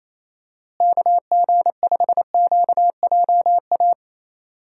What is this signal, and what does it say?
Morse code, heard as a keyed tone, about 28 wpm, 700 Hz. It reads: KG5QJA
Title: USA Call Signs - 6 Characters - Repeated 3x - 28wpm